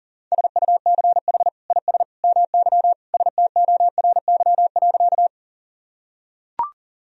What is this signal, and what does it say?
Morse code, heard as a keyed tone, about 40 wpm, 700 Hz. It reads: SUCH IS MY STORY.
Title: Sentences from Top 1000 Words - Review Part 1 - 40wpm